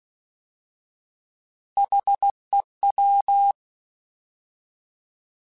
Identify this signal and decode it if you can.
Morse code, heard as a keyed tone, about 16 wpm, 800 Hz. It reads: HEW
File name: Alphabetical / Alphabetical mp3s / Hs @15